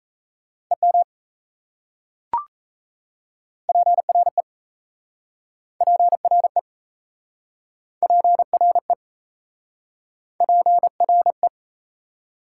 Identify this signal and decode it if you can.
Morse code, heard as a keyed tone, about 38 wpm, 700 Hz. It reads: EM PRE PRE PRE PRE